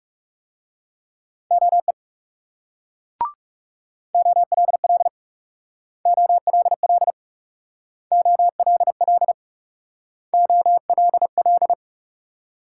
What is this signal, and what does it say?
Morse code, heard as a keyed tone, about 38 wpm, 700 Hz. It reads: OE OLL OLL OLL OLL
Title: Speed-Racing Multi-letter Phonemes - 30wpm